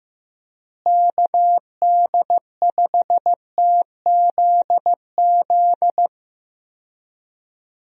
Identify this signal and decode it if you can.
Morse code, heard as a keyed tone, about 15 wpm, 700 Hz. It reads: KD5TZZ
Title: USA Call Signs - 6 Characters - Repeated 2x - 15wpm